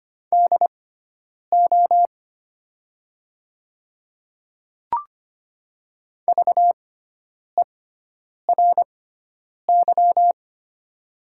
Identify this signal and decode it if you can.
Morse code, heard as a keyed tone, about 25 wpm, spaced as for 9 wpm, 700 Hz. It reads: DO VERY
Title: 6x Character Spacing - Top 500 Words - 25wpm